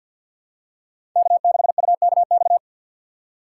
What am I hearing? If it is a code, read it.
Morse code, heard as a keyed tone, about 53 wpm, 700 Hz. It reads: K6UKX
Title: USA Call Signs - 4x Character Spacing - 50wpm